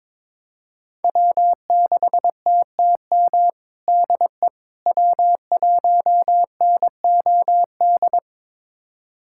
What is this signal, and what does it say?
Morse code, heard as a keyed tone, about 22 wpm, 700 Hz. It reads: W6TTM DE W1NOD
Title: USA Callsign DE Callsign - 5 Characters - 22wpm